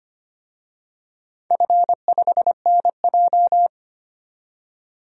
Morse code, heard as a keyed tone, about 25 wpm, 700 Hz. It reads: F5NJ
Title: DXCC-100 Call Signs - 4 Characters - 25wpm